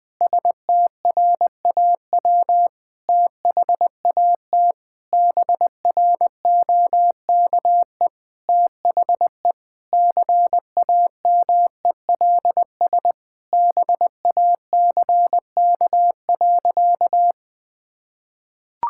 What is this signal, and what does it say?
Morse code, heard as a keyed tone, about 20 wpm, 700 Hz. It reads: STRAW THAT BROKE THE CAMELS BACK.
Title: Sentences with 8-Word Phrases - 20wpm